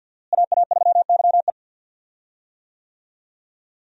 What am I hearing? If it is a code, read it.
Morse code, heard as a keyed tone, about 51 wpm, 700 Hz. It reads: AA3YE
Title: USA Call Signs - 5 Characters - Repeated 3x - 50wpm